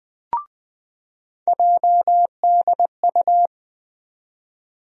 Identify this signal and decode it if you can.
Morse code, heard as a keyed tone, about 20 wpm, 700 Hz. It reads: JDU
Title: USA Call Sign Postfixes - 3 Letters 20wpm